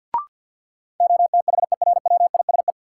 Morse code, heard as a keyed tone, about 46 wpm, 700 Hz. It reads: OTHERWISE
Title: Sentences from Top 1200 Words - New Word Drill Alt - 50wpm